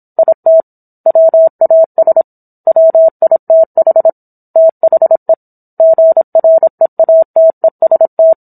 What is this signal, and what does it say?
Morse code, heard as a keyed tone, about 27 wpm, 650 Hz. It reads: IT WAS WITH THE GREATEST